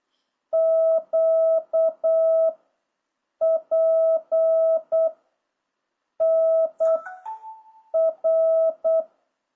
Morse code, heard as a keyed tone, about 8 wpm, 660 Hz. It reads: Q P N R